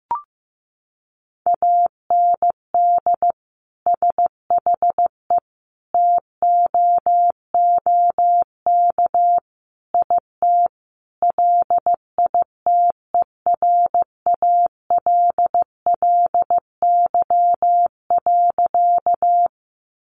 Morse code, heard as a keyed tone, about 15 wpm, 700 Hz. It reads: AND SHE TOOK IT LITERALLY.
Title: Sentences from Top 1400 Words 15wpm